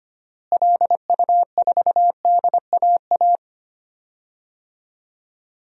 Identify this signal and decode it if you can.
Morse code, heard as a keyed tone, about 25 wpm, 700 Hz. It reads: LU4DAA
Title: DXCC-100 Call Signs - 6 Characters - Repeated 3x - 25wpm